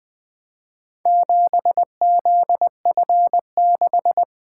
Morse code, heard as a keyed tone, about 21 wpm, 700 Hz. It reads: 7ZF6
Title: Four Letter-Number Combinations - 20wpm